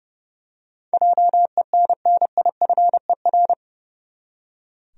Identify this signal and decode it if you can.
Morse code, heard as a keyed tone, about 29 wpm, 700 Hz. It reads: JENNIFER
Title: USA Common Names - 30wpm